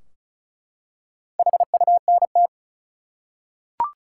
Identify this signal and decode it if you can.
Morse code, heard as a keyed tone, about 36 wpm, 700 Hz. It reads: HUNT
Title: Four Letter Words - 35wpm